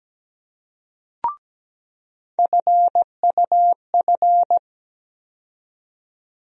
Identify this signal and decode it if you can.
Morse code, heard as a keyed tone, about 17 wpm, 700 Hz. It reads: FUF
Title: USA Call Sign Postfixes - 3 Letters 17wpm